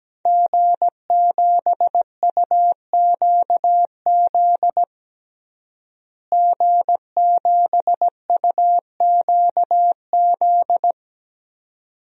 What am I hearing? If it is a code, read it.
Morse code, heard as a keyed tone, about 17 wpm, 700 Hz. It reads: G7UQZ G7UQZ
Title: DXCC-100 Call Signs - 5 Characters - Repeated 2x - 17wpm